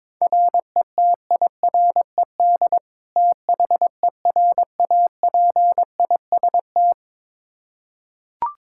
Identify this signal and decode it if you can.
Morse code, heard as a keyed tone, about 22 wpm, 700 Hz. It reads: RETIRED THERAPIST
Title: Practice QSO - Exchange 3 - More Info - 22wpm